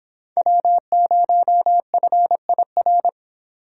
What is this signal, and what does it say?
Morse code, heard as a keyed tone, about 25 wpm, 700 Hz. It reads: W0FIR